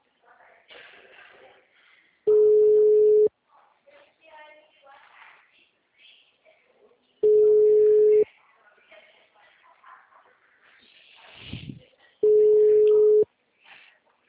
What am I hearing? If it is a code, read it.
Ringback tone (Europe, 425 Hz), with speech.
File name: out- -2007- - - .90068